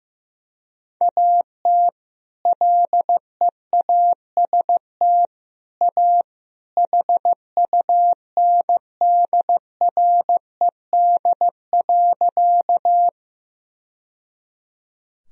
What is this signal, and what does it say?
Morse code, heard as a keyed tone, about 16 wpm, 700 Hz. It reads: AT LEAST A HUNDRED.
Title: Sentences from Top 1000 Words - Review Part 2 - 15wpm